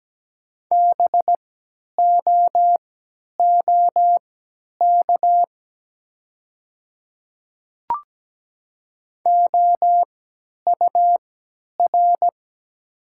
Morse code, heard as a keyed tone, about 17 wpm, spaced as for 10 wpm, 700 Hz. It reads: BOOK OUR
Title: 3x Character Spacing - Top 300 Words - 17wpm